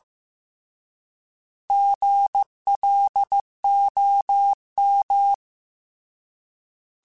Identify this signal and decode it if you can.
Morse code, heard as a keyed tone, about 15 wpm, 800 Hz. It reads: GLOM
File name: Alphabetical / Alphabetical mp3s / Gs @13